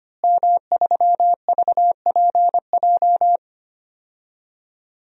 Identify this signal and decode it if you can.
Morse code, heard as a keyed tone, about 25 wpm, 700 Hz. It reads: M3VPJ
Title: DXCC-100 Call Signs - 5 Characters - Repeated 2x - 25wpm